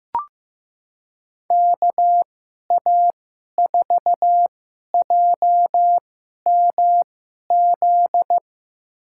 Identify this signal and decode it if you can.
Morse code, heard as a keyed tone, about 15 wpm, spaced as for 11 wpm, 700 Hz. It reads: KA4JMZ
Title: USA Call Signs - 6 Characters - 2x Character Spacing - 15wpm